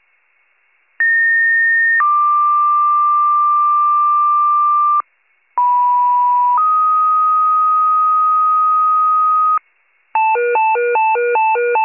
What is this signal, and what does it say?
Two-tone page, heard as a one-second tone, then a three-second tone, about 1850 then 1180 Hz, about 980 then 1250 Hz. It is heard from 1.0 to 9.6 s.